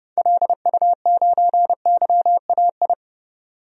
Morse code, heard as a keyed tone, about 32 wpm, 700 Hz. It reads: LU9YAI
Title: DXCC-100 Call Signs - 6 Characters - 30wpm